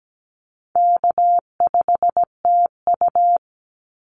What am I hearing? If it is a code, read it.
Morse code, heard as a keyed tone, about 17 wpm, 700 Hz. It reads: K5TU
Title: Easy Call Signs Repeated 2x - 17wpm